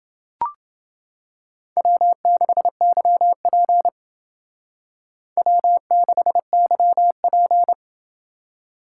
Morse code, heard as a keyed tone, about 28 wpm, 700 Hz. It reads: W6YP W6YP